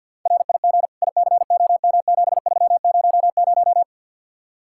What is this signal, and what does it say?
Morse code, heard as a keyed tone, about 57 wpm, 700 Hz. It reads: RIG ICOM7300
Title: Speed-Racing Common QSO Phrases - 50wpm